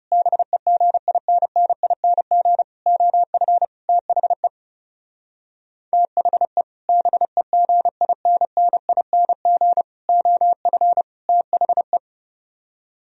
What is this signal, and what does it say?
Morse code, heard as a keyed tone, about 36 wpm, 700 Hz. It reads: BEGINNING OF THE THE BEGINNING OF THE
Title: Speed-Racing - Sets of 4 Words 30wpm